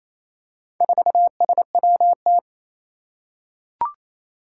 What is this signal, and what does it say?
Morse code, heard as a keyed tone, about 27 wpm, 700 Hz. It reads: 4SWT
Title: Four Challenging Letter-Number Combinations 28wpm